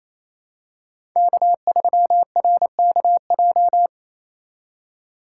Morse code, heard as a keyed tone, about 28 wpm, 700 Hz. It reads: K3RKJ
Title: USA Call Signs - 5 Characters - 4x Character Spacing - 28wpm